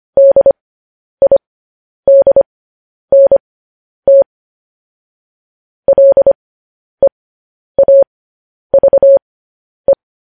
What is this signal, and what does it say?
Morse code, heard as a keyed tone, about 25 wpm, spaced as for 10 wpm, 550 Hz. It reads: DIDNT LEAVE
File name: podcast / News headlines-1x-25x10